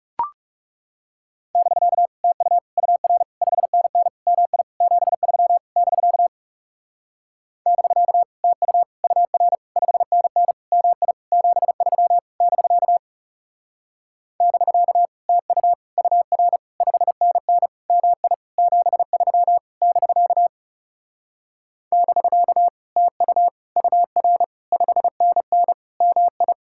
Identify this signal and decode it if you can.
Morse code, heard as a keyed tone, about 41 wpm, 700 Hz. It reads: <BK> TU UR 5NN MI 73 <BK> <BK> TU UR 5NN MI 73 <BK> <BK> TU UR 5NN MI 73 <BK> <BK> TU UR 5NN MI